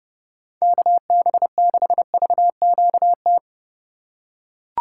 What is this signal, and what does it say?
Morse code, heard as a keyed tone, about 30 wpm, 700 Hz. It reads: KB6VQT
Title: USA Call Signs - 6 Characters - 3x Character Spacing - 30wpm